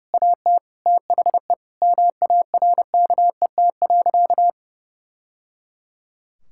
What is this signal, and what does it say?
Morse code, heard as a keyed tone, about 30 wpm, 700 Hz. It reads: AT THE MARKET.
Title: Sentences from Top 1000 Words - Review Part 6 - 30wpm